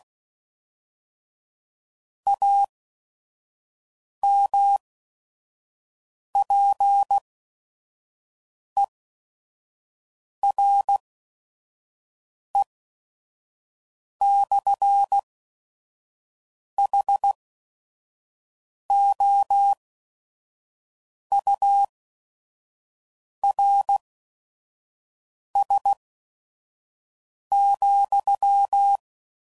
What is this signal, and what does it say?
Morse code, heard as a keyed tone, about 16 wpm, spaced as for 5 wpm, 800 Hz. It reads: AMPERE/HOURS,
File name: CW Code Groups @5WPM / Technical Subjects / SwPwrSup